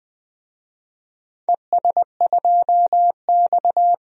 Morse code, heard as a keyed tone, about 20 wpm, 700 Hz. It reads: ES2X
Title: DXCC-100 Call Signs - 4 Characters - 20wpm